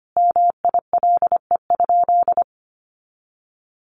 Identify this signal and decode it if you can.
Morse code, heard as a keyed tone, about 25 wpm, 700 Hz. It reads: MILE?
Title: Sentences from Top 400 Words 25wpm